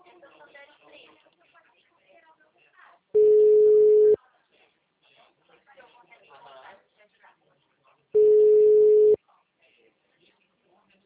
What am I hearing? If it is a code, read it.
Ringback tone (Europe, 425 Hz), with speech.